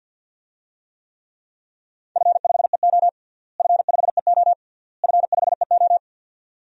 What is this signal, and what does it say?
Morse code, heard as a keyed tone, about 49 wpm, 700 Hz. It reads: F5EO F5EO F5EO
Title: DXCC-100 Call Signs - 4 Characters - Repeated 3x - 50wpm